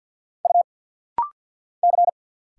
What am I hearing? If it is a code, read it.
Morse code, heard as a keyed tone, about 50 wpm, 700 Hz. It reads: U C
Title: Single Letter-Number Alt - Rapid-Fire 50wpm